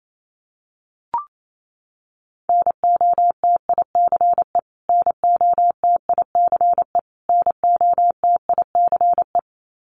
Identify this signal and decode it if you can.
Morse code, heard as a keyed tone, about 28 wpm, 700 Hz. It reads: NOTICE NOTICE NOTICE
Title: Top 500 Words Repeated 28wpm